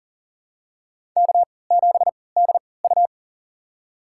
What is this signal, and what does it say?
Morse code, heard as a keyed tone, about 40 wpm, spaced as for 23 wpm, 700 Hz. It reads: K7DU
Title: USA Call Signs - 4 Characters - 3x Character Spacing - 40wpm